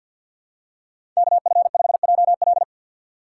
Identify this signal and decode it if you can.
Morse code, heard as a keyed tone, about 51 wpm, 700 Hz. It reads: KF5JL
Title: USA Call Signs - 5 Characters - 4x Character Spacing - 50wpm